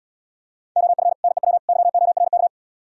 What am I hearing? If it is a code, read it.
Morse code, heard as a keyed tone, about 77 wpm, 700 Hz. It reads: OF NEW YORK